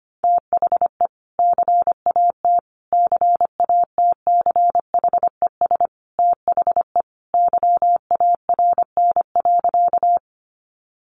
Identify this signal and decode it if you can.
Morse code, heard as a keyed tone, about 25 wpm, 700 Hz. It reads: THE CAT CATCHES THE YARN.
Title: Additional Easy Phrases 25wpm